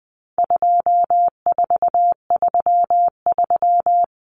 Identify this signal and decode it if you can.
Morse code, heard as a keyed tone, about 20 wpm, 700 Hz. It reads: 2433
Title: Common Cut Numbers - 4-Digit 20wpm